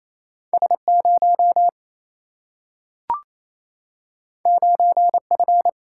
Morse code, heard as a keyed tone, about 28 wpm, 700 Hz. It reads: S 0 9 F
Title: Single-Letter Single-Number 28wpm